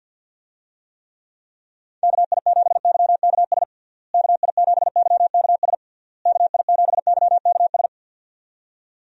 Morse code, heard as a keyed tone, about 52 wpm, 700 Hz. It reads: KI7YKS KI7YKS KI7YKS
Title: USA Call Signs - 6 Characters - Repeated 3x - 50wpm